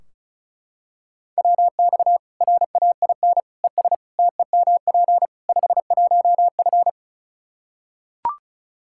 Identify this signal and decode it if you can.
Morse code, heard as a keyed tone, about 35 wpm, 700 Hz. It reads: WX RAIN ES TEMP 51F
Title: Practice QSO - Exchange 2 - Info - 35wpm